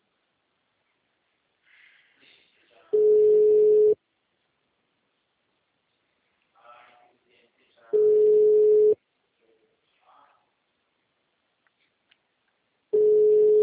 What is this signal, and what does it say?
Ringback tone (Europe, 425 Hz), with speech.